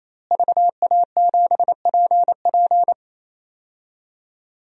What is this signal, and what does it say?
Morse code, heard as a keyed tone, about 28 wpm, 700 Hz. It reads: VA7PP
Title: DXCC-100 Call Signs - 5 Characters - Repeated 3x - 28wpm